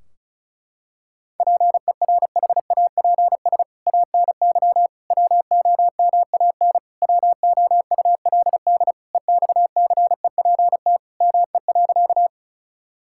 Morse code, heard as a keyed tone, about 34 wpm, 700 Hz. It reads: PERHAPS ANY WOMAN WOULD EXCEPT ME.
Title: Sentences from Top 1600 Words 35wpm